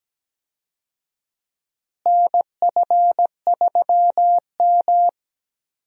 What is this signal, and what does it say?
Morse code, heard as a keyed tone, about 17 wpm, 700 Hz. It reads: NF3M
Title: USA Call Signs - 4 Characters - Repeated 2x - 17wpm